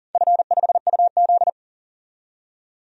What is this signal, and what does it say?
Morse code, heard as a keyed tone, about 40 wpm, 700 Hz. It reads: F5UZ
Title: DXCC-100 Call Signs - 4 Characters - 40wpm